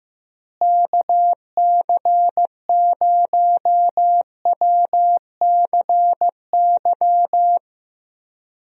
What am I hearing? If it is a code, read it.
Morse code, heard as a keyed tone, about 15 wpm, 700 Hz. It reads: KC0WCY
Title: USA Call Signs - 6 Characters - Repeated 3x - 15wpm